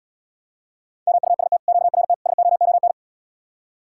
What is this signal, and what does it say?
Morse code, heard as a keyed tone, about 80 wpm, 700 Hz. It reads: MUST QRT SOON